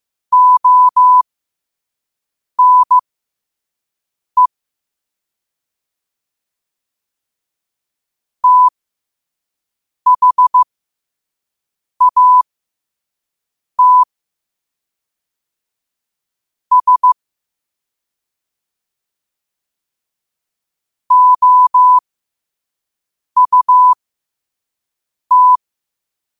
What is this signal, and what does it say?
Morse code, heard as a keyed tone, about 15 wpm, spaced as for 5 wpm, 1000 Hz. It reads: ONE THAT S OUT
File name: qotdfiles / 5wpm